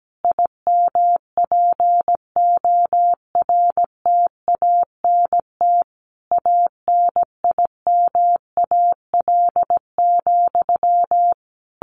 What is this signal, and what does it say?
Morse code, heard as a keyed tone, about 17 wpm, 700 Hz. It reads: IMPORTANT ANIMAL,
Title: A Sound Of Thunder 17wpm